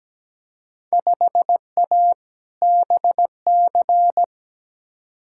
Morse code, heard as a keyed tone, about 17 wpm, 700 Hz. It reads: 5A BC
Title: Field Day Exchanges - 17wpm